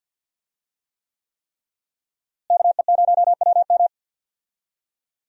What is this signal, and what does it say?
Morse code, heard as a keyed tone, about 53 wpm, 690 Hz. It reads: KE0WM
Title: USA Call Signs - 5 Characters - Repeated 3x - 50wpm